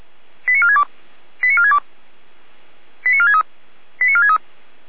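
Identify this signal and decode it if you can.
Five-tone selective call (ZVEI), heard as quick runs of five tones, tones of about 70 ms. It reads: 89462 89462 89463 89463
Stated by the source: ZVEI five-tone call 89436 received by SDR.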